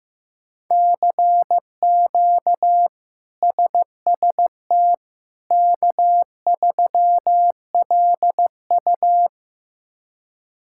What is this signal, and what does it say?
Morse code, heard as a keyed tone, about 15 wpm, 700 Hz. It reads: CQ SST K3LU
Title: K1USN SST QSO Practice - 15wpm